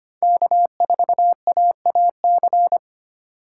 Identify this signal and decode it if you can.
Morse code, heard as a keyed tone, about 25 wpm, 700 Hz. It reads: K4AAC
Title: K1USN SST QSO Practice - 25wpm